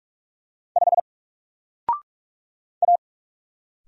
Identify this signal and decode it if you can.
Morse code, heard as a keyed tone, about 46 wpm, 700 Hz. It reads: 5 A
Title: Single Letter-Number Alt 45wpm